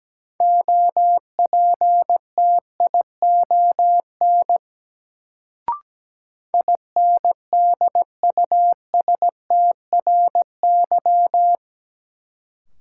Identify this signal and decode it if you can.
Morse code, heard as a keyed tone, about 17 wpm, 700 Hz. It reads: OPTION INDUSTRY
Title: Sentences from Top 1300 Words - New Word Drill Alt - 17wpm